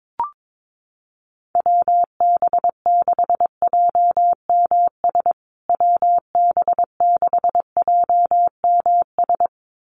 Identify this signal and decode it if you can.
Morse code, heard as a keyed tone, about 22 wpm, 700 Hz. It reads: WB6JMS WB6JMS
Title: USA Call Signs Repeated 2x 22wpm